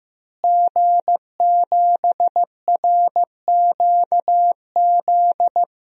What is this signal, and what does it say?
Morse code, heard as a keyed tone, about 15 wpm, 700 Hz. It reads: G7RQZ
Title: DXCC-100 Call Signs - 5 Characters - Repeated 2x - 15wpm